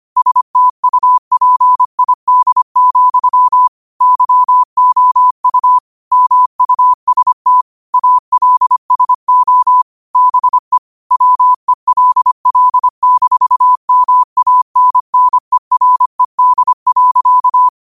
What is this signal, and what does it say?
Morse code, heard as a keyed tone, about 25 wpm, 1000 Hz. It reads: STUPID, YOU MUST ALSO BE WELL-MANNERED.